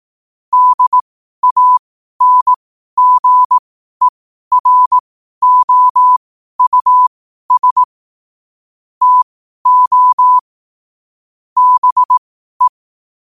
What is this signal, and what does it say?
Morse code, heard as a keyed tone, about 18 wpm, spaced as for 12 wpm, 1000 Hz. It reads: DANGEROUS TO BE
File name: qotdfiles / 12wpm